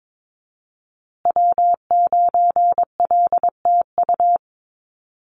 Morse code, heard as a keyed tone, about 22 wpm, 700 Hz. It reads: W9LTU
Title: USA Call Signs Repeated 3x 22wpm